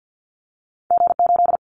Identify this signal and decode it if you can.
Morse code, heard as a keyed tone, about 50 wpm, 700 Hz. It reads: G8
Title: Callsign Prefixes for Top 25 DXCC - Repeated 3x - 50wpm